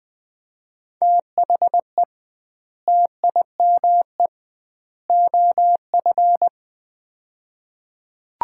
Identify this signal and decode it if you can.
Morse code, heard as a keyed tone, about 20 wpm, 700 Hz. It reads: THE TIME OF
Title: 2x Word-Spacing - Sets of 3 Words - 20wpm